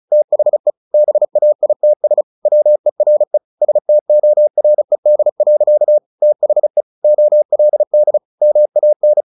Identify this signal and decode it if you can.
Morse code, heard as a keyed tone, about 34 wpm, 600 Hz. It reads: THE BAITS WERE STORED. THE OLD MAN
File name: the old man and the sea / the old man and the sea 1 35wpm